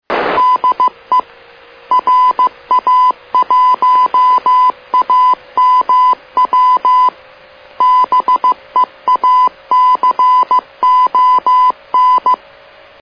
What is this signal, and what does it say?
Morse code, heard as a keyed tone, about 15 wpm, 1000 Hz, with radio noise. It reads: DE RA1AMW BEACON
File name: srp / mf / beacon